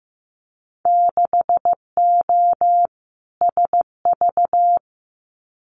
Morse code, heard as a keyed tone, about 15 wpm, 700 Hz. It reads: 6O SV
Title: Winter Field Day Exchanges Encore 15wpm